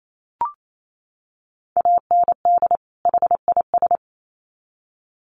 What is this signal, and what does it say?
Morse code, heard as a keyed tone, about 28 wpm, 700 Hz. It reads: AND HIS
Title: Sets of 2 Words 28wpm